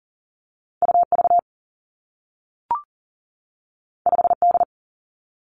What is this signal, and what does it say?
Morse code, heard as a keyed tone, about 38 wpm, 700 Hz. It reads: U V 5 D
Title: Two Challenging Letter-Number Combinations 40wpm